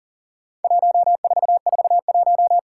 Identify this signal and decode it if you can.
Morse code, heard as a keyed tone, about 41 wpm, 700 Hz. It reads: 1441
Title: Worked All Europe DX Contest - QTC Exchange - 40wpm